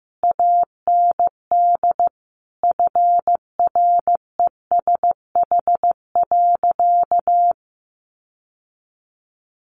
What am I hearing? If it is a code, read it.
Morse code, heard as a keyed tone, about 15 wpm, 700 Hz. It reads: AND FRESH.
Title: Sentences from Top 1200 Words 15wpm